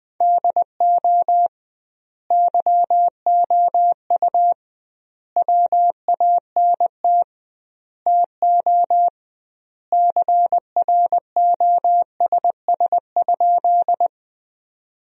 Morse code, heard as a keyed tone, about 20 wpm, 700 Hz. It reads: DO YOU WANT TO CROSS?
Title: 2x Word-Spacing - Sentences from Top 300 Words - 20wpm